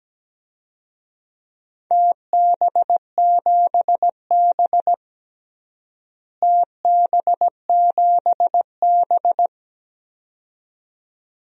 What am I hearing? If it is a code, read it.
Morse code, heard as a keyed tone, about 17 wpm, 700 Hz. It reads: TB7B TB7B
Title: DXCC-100 Call Signs - 4 Characters - Repeated 2x - 17wpm